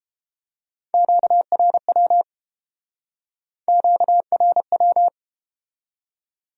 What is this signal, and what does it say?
Morse code, heard as a keyed tone, about 31 wpm, 700 Hz. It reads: QRW QRW